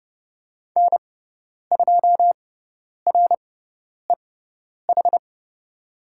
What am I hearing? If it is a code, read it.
Morse code, heard as a keyed tone, about 30 wpm, spaced as for 10 wpm, 700 Hz. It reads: N2REH
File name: Lesson 17g - Letter u - Call Signs 30wpm